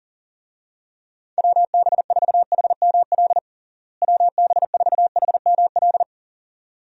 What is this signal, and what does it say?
Morse code, heard as a keyed tone, about 40 wpm, 700 Hz. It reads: WB4HML WB4HML
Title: USA Call Signs - 6 Characters - Repeated 2x - 40wpm